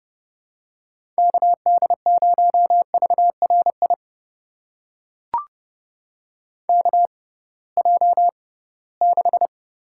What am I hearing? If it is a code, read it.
Morse code, heard as a keyed tone, about 32 wpm, spaced as for 20 wpm, 700 Hz. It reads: KD0VRI KJ6
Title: USA Call Signs - 6 Characters - 6x Character Spacing - 30wpm